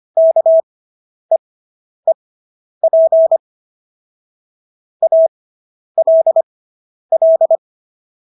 Morse code, heard as a keyed tone, about 25 wpm, spaced as for 10 wpm, 650 Hz. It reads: KEEP ALL